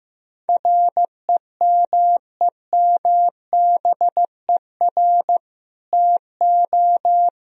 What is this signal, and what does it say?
Morse code, heard as a keyed tone, about 15 wpm, 700 Hz. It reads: REMEMBER TO
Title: Sentences from Top 1000 Words - Review Part 3 - 15wpm